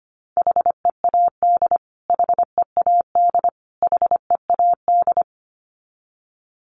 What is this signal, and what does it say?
Morse code, heard as a keyed tone, about 25 wpm, 700 Hz. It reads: HEAD HEAD HEAD
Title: Top 300 Words Repeated 25wpm